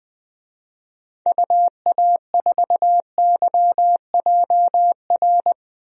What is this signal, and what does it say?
Morse code, heard as a keyed tone, about 21 wpm, 700 Hz. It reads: UA4YJR
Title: DXCC-100 Call Signs - 6 Characters - Repeated 3x - 20wpm